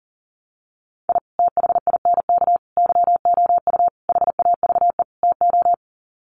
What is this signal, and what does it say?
Morse code, heard as a keyed tone, about 40 wpm, 700 Hz. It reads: I THINK YOU HAVE TO
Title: Sets of 5 Words 40wpm